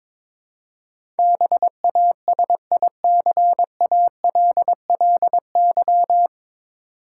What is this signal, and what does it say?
Morse code, heard as a keyed tone, about 22 wpm, 700 Hz. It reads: BASICALLY
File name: 9 Letter Words - First 3 Letters Given - 22wpm